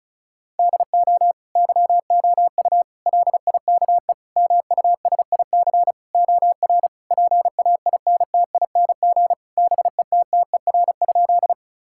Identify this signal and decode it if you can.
Morse code, heard as a keyed tone, about 36 wpm, 700 Hz. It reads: DO YOU LIKE MUSIC OR PAINTING BETTER?
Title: Sentences from Top 1000 Words - Review Part 6 - 35wpm